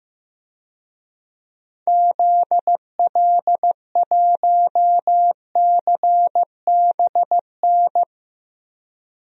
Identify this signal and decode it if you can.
Morse code, heard as a keyed tone, about 15 wpm, 700 Hz. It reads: ZL1CBN
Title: DXCC-100 Call Signs - 6 Characters - Repeated 3x - 15wpm